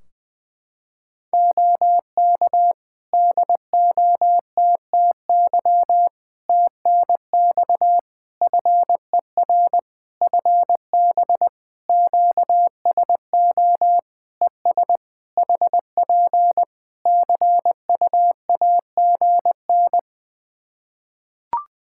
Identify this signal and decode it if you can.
Morse code, heard as a keyed tone, about 20 wpm, 700 Hz. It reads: OK DOTTY TNX FER FB QSO ES HP CUAGN
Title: K1USN SST QSO Practice 20wpm